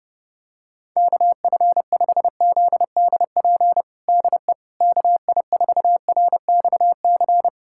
Morse code, heard as a keyed tone, about 30 wpm, 700 Hz. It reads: KF5ZDP DE KI4RXC
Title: USA Callsign DE Callsign 30wpm